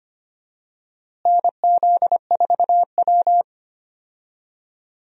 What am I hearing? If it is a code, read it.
Morse code, heard as a keyed tone, about 25 wpm, 700 Hz. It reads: NZ4W
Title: USA Call Signs - 4 Characters - Repeated 3x - 25wpm